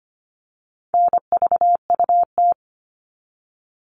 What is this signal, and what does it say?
Morse code, heard as a keyed tone, about 25 wpm, 700 Hz. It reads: NVUT
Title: All Cut Numbers - 4-Digit 25wpm